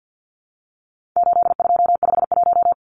Morse code, heard as a keyed tone, about 50 wpm, 700 Hz. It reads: 8251
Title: Numbers - 4-Digit 50wpm